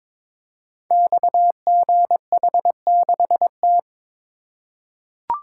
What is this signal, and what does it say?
Morse code, heard as a keyed tone, about 22 wpm, 700 Hz. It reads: XGH6T
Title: Code Groups - 5 Letter:Numbers - 22wpm